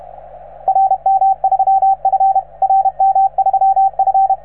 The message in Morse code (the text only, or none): RM3F RM3F